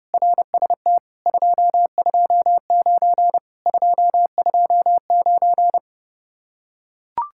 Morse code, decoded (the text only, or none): RST 229 229